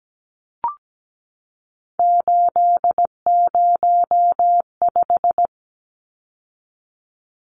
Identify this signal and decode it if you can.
Morse code, heard as a keyed tone, about 17 wpm, 700 Hz. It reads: 805
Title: Numbers - 3-Digit 17wpm